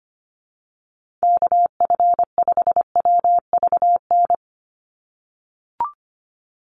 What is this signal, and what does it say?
Morse code, heard as a keyed tone, about 25 wpm, 700 Hz. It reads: KF5WVN
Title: USA Call Signs 25wpm